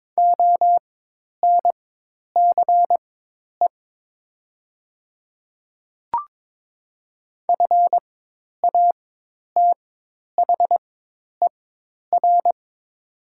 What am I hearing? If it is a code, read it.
Morse code, heard as a keyed tone, about 22 wpm, spaced as for 10 wpm, 700 Hz. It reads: ONCE FATHER
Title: 4x Character Spacing - Top 300 Words - 22wpm